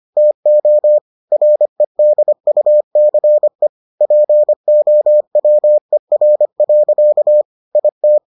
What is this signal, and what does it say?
Morse code, heard as a keyed tone, about 25 wpm, 600 Hz. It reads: TO REDUCE POWER. IT